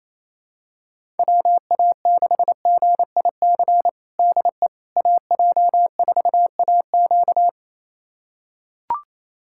Morse code, heard as a keyed tone, about 28 wpm, 700 Hz. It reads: WA6GIC DE AJ4AQ
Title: USA Callsign DE Callsign 28wpm